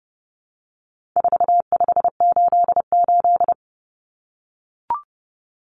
Morse code, heard as a keyed tone, about 30 wpm, 700 Hz. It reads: 4588
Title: Common Cut Numbers - 4-Digit 30wpm